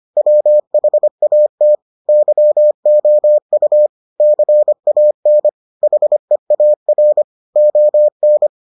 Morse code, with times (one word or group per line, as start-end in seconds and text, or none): 0.2-1.8 WHAT
2.1-3.9 YOU
4.2-5.5 CAN
5.8-7.2 HEAR
7.6-8.5 ON